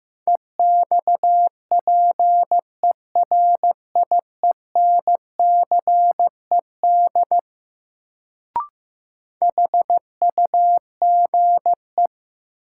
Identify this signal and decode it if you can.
Morse code, heard as a keyed tone, about 15 wpm, 700 Hz. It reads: EXPERIENCED HUGE
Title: Sentences from Top 1100 Words - New Word Drill - 15wpm